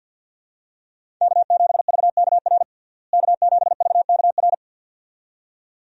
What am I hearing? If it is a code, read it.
Morse code, heard as a keyed tone, about 46 wpm, 700 Hz. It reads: K7VKR K7VKR
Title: USA Call Signs - 5 Characters - Repeated 2x - 50wpm